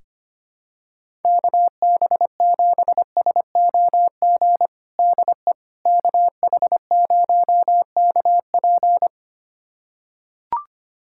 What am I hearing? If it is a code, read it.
Morse code, heard as a keyed tone, about 25 wpm, 700 Hz. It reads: KB7SOG DE KH0KP
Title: USA Callsign DE Callsign 25wpm